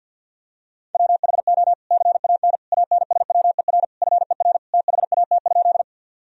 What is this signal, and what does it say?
Morse code, heard as a keyed tone, about 47 wpm, 700 Hz. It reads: WHO CAN ANSWER FER THAT?
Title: Sentences from Top 300 Words and Abbreviations 50wpm